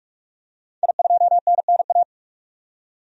Morse code, heard as a keyed tone, about 46 wpm, 700 Hz. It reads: I2NNA
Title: DXCC-100 Call Signs - 5 Characters - 45wpm